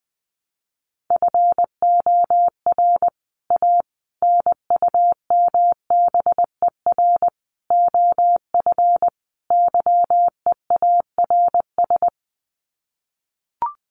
Morse code, heard as a keyed tone, about 20 wpm, 700 Hz. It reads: FOR A NUMBER OF YEARS
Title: Sets of 5 Words Encore 20wpm